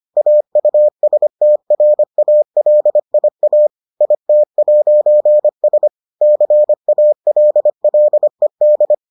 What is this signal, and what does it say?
Morse code, heard as a keyed tone, about 25 wpm, 600 Hz. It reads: AUSTRALIA IT'S CALLED